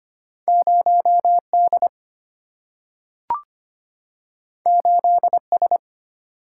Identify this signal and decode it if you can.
Morse code, heard as a keyed tone, about 25 wpm, 700 Hz. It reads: 0D 8S
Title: Single-Letter Single-Number 25wpm